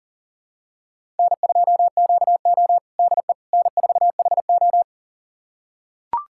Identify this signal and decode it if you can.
Morse code, heard as a keyed tone, about 40 wpm, 700 Hz. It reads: N2QO DE N4HO
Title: USA Callsign DE Callsign - 4 Characters - 40wpm